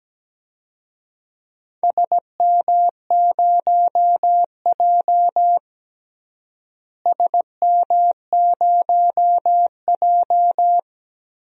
Morse code, heard as a keyed tone, about 17 wpm, 700 Hz. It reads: SM0J SM0J
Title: DXCC-100 Call Signs - 4 Characters - Repeated 3x - 17wpm